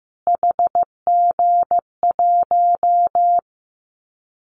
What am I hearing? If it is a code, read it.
Morse code, heard as a keyed tone, about 15 wpm, 700 Hz. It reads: HG1
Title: Callsign Prefixes for Top 25 DXCC - Repeated 3x - 15wpm